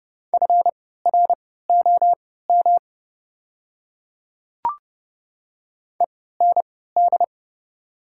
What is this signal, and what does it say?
Morse code, heard as a keyed tone, about 30 wpm, spaced as for 17 wpm, 690 Hz. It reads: FROM END
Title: 3x Character Spacing - Top 200 Words - 30wpm